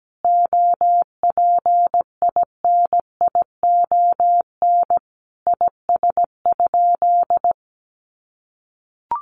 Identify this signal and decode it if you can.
Morse code, heard as a keyed tone, about 17 wpm, 700 Hz. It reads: OPINION IS?
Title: Sentences from Top 1200 Words 17wpm